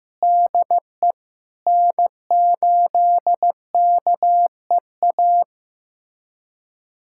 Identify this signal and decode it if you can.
Morse code, heard as a keyed tone, about 16 wpm, 700 Hz. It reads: DE N8KEA
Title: USA Callsign DE Callsign - 5 Characters - 15wpm